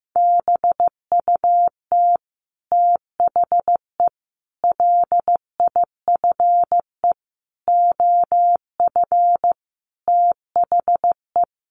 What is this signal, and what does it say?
Morse code, heard as a keyed tone, about 15 wpm, 700 Hz. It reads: BUT THE LIFE OF THE
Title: Sentences from Top 1500 Words 15wpm